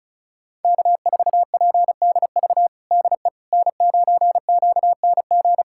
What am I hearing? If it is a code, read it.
Morse code, heard as a keyed tone, about 36 wpm, 700 Hz. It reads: K4PDV DE N9QNG
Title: USA Callsign DE Callsign 35wpm